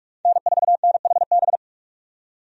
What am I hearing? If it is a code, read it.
Morse code, heard as a keyed tone, about 47 wpm, 700 Hz. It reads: N4NHB
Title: USA Call Signs - 5 Characters - 45wpm